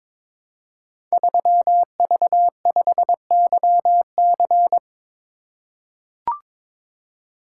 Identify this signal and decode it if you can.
Morse code, heard as a keyed tone, about 22 wpm, 700 Hz. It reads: 3V5YC
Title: Code Groups - 5 Letter:Numbers - 22wpm